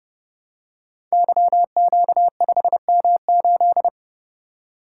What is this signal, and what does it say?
Morse code, heard as a keyed tone, about 31 wpm, 700 Hz. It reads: YQ5M8
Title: Code Groups - 5 Letter:Numbers - 30wpm